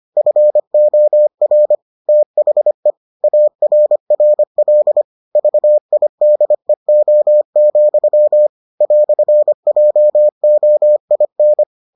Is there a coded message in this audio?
Yes: Morse code from 0.2 to 11.6 s, heard as a keyed tone, about 25 wpm, 600 Hz. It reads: FOR THE ARRL VIDEO, "JOIN